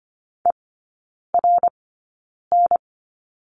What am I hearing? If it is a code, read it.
Morse code, heard as a keyed tone, about 25 wpm, spaced as for 9 wpm, 700 Hz. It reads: ERN
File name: course / 25:10wpm / 093 Lesson 16e - Letter c - Three Characters 25wpm